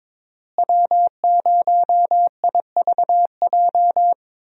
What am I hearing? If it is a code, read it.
Morse code, heard as a keyed tone, about 22 wpm, 700 Hz. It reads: W0IVJ
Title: USA Call Signs - 5 Characters - Repeated 2x - 22wpm